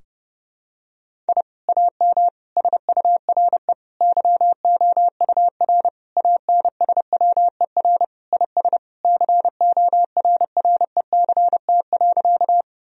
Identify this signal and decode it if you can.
Morse code, heard as a keyed tone, about 30 wpm, 700 Hz. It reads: I AM SURE YOUR ANSWER IS CORRECT.
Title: Sentences from Top 1000 Words - Review Part 2 - 30wpm